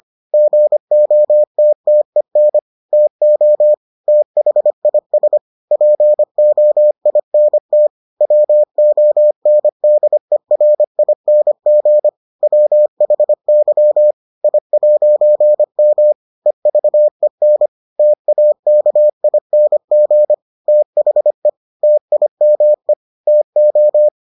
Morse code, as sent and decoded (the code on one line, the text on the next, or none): --. --- - - . -. / - --- / - .... .. ... / .--. --- .. -. - / .-- --- -. -.. .-. .. -. --. / .-- .... -.-- / .. .----. -- / . ...- . -. / - .- -.- .. -. --. / - .... / - .. -- . / - ---
GOTTEN TO THIS POINT WONDERING WHY I'M EVEN TAKING THE TIME TO